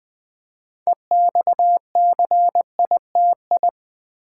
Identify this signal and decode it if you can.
Morse code, heard as a keyed tone, about 20 wpm, 700 Hz. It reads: EXCITI
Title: Build A Word - 3 Letters At A Time - 8-Letter Words - 20wpm